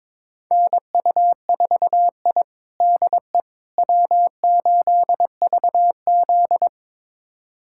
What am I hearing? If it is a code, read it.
Morse code, heard as a keyed tone, about 22 wpm, 700 Hz. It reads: NU4I DE W8VZ
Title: USA Callsign DE Callsign - 4 Characters - 22wpm